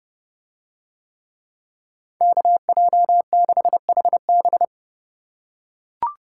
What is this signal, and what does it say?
Morse code, heard as a keyed tone, about 28 wpm, 700 Hz. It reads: KJ6HB